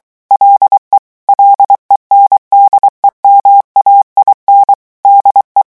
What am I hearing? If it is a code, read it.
Morse code, heard as a keyed tone, about 23 wpm, 800 Hz. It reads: LE LENDEMAIN DE
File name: morse mp3 / revision9 citations humoristiques vitesse 22 mots minute